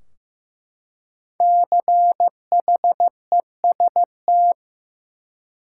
Morse code, heard as a keyed tone, about 16 wpm, 700 Hz. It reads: CHEST
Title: Five Letter Words - 15wpm